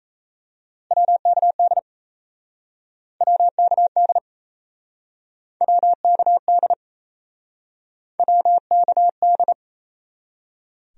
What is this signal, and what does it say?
Morse code, heard as a keyed tone, about 36 wpm, 700 Hz. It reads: WKD WKD WKD WKD